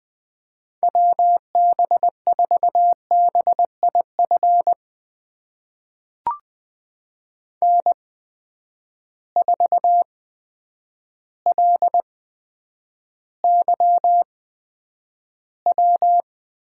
Morse code, heard as a keyed tone, about 20 wpm, spaced as for 9 wpm, 700 Hz. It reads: WB4BIF N4LYW
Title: USA Call Signs - 8x Character Spacing - 20wpm